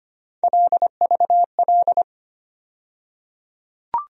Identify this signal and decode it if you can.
Morse code, heard as a keyed tone, about 25 wpm, 700 Hz. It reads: LVL
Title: USA Call Sign Postfixes - 3 Letters 25wpm